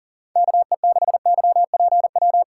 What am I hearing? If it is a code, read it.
Morse code, heard as a keyed tone, about 40 wpm, 700 Hz. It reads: KE6YPW
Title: USA Callsign DE Callsign - 6 Characters - 40wpm